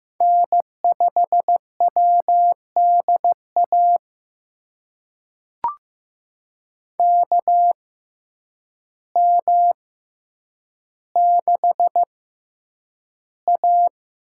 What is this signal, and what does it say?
Morse code, heard as a keyed tone, about 15 wpm, spaced as for 8 wpm, 700 Hz. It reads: N5WDA KM6A
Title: USA Call Signs - 6x Character Spacing - 15wpm